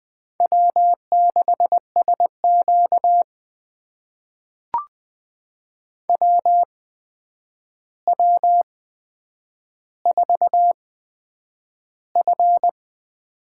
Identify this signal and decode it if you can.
Morse code, heard as a keyed tone, about 21 wpm, spaced as for 9 wpm, 700 Hz. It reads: W6SQ WW4F
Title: USA Call Signs - 4 Characters - 8x Character Spacing - 20wpm